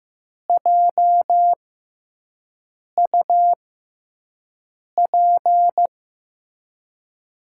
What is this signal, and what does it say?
Morse code, heard as a keyed tone, about 15 wpm, spaced as for 5 wpm, 700 Hz. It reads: JUP
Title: USA Call Signs - 6 Characters - 6x Character Spacing - 15wpm